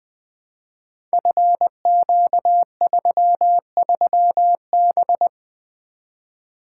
Morse code, heard as a keyed tone, about 20 wpm, 700 Hz. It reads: FQ33B
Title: Code Groups - 5 Letter:Numbers - 20wpm